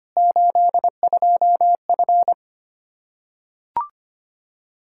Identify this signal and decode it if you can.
Morse code, heard as a keyed tone, about 25 wpm, 700 Hz. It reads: 82F
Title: K1USN SST QSO Practice 25wpm